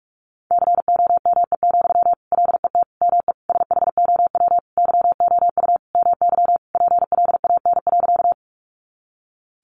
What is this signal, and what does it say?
Morse code, heard as a keyed tone, about 44 wpm, 700 Hz. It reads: COME, LET ME SHOW YOU MY PLAN.
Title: Sentences from Top 500 Words 45wpm